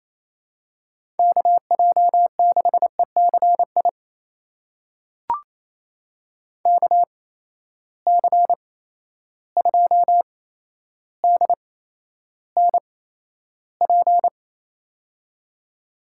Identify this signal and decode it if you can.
Morse code, heard as a keyed tone, about 28 wpm, spaced as for 12 wpm, 700 Hz. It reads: KJ6ECI KC2DNP
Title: USA Call Signs - 6 Characters - 8x Character Spacing - 28wpm